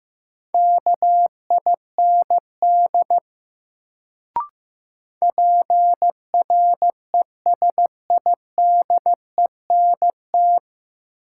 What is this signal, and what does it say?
Morse code, heard as a keyed tone, about 15 wpm, 700 Hz. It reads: KIND PRESIDENT
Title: Sets of 2 Words - New Word Drill - 15wpm